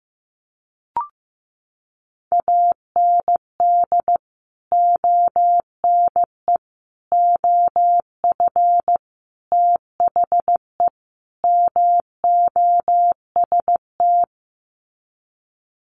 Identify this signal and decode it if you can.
Morse code, heard as a keyed tone, about 15 wpm, 700 Hz. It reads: AND ONE OF THE MOST
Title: Sets of 5 Words Encore 15wpm